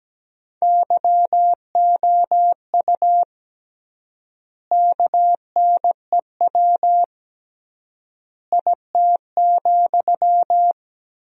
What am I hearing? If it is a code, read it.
Morse code, heard as a keyed tone, about 17 wpm, 700 Hz. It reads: YOU KNEW IT,
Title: 3x Word-Spacing - Sentences from Top 500 Words - 17wpm